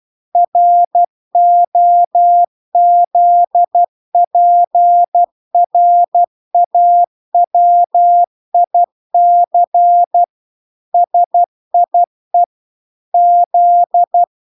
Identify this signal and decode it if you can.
Morse code, heard as a keyed tone, about 12 wpm, 700 Hz. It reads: ROZPRAWIC SIE Z